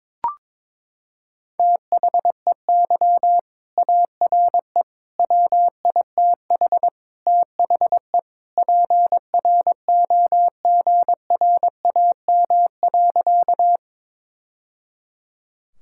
Morse code, heard as a keyed tone, about 22 wpm, 700 Hz. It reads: THEY ARE WITH THE PROGRAM.
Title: Sentences from Top 1000 Words - Review Part 5 - 22wpm